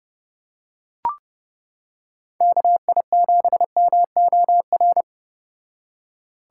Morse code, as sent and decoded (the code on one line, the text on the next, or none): -.- .. --... -- --- .-.
KI7MOR